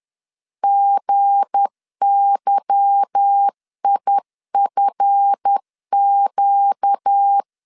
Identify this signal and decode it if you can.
Morse code, heard as a keyed tone, about 10 wpm, 800 Hz. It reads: GYIFQ
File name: LEZIONI / 8 YJ45 / 8 Y 10wpm 2615